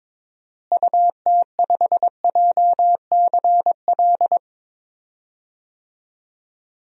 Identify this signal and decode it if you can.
Morse code, heard as a keyed tone, about 21 wpm, 700 Hz. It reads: UT5JCL
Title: DXCC-100 Call Signs - 6 Characters - Repeated 2x - 22wpm